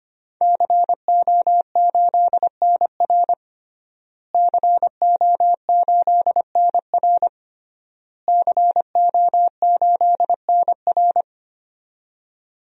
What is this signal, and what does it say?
Morse code, heard as a keyed tone, about 25 wpm, 700 Hz. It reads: CO8NR CO8NR CO8NR
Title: DXCC-100 Call Signs - 5 Characters - Repeated 3x - 25wpm